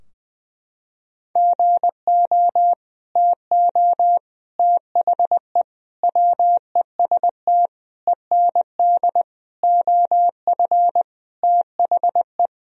Morse code, heard as a keyed tone, about 19 wpm, 700 Hz. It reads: GO TO THE WEST END OF THE
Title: Sentences from Top 1000 Words - Review Part 2 - 20wpm